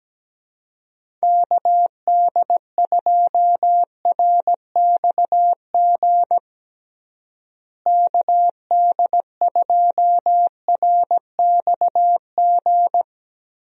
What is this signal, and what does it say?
Morse code, heard as a keyed tone, about 17 wpm, 700 Hz. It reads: KD2RXG KD2RXG
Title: USA Call Signs - 6 Characters - Repeated 2x - 17wpm